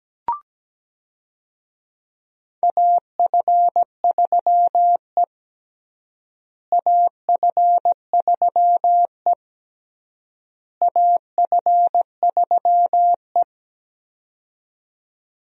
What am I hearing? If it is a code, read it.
Morse code, heard as a keyed tone, about 17 wpm, 700 Hz. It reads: AF3E AF3E AF3E
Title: USA Call Signs - 4 Characters - Repeated 3x - 17wpm